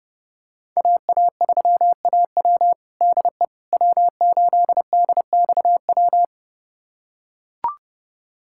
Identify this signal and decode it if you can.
Morse code, heard as a keyed tone, about 32 wpm, 700 Hz. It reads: AA3AW DE W8DXW
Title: USA Callsign DE Callsign - 5 Characters - 30wpm